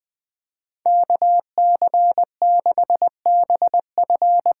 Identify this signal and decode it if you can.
Morse code, heard as a keyed tone, about 20 wpm, 700 Hz. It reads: KC6BF
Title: USA Call Signs - 6 Characters - 8x Character Spacing - 20wpm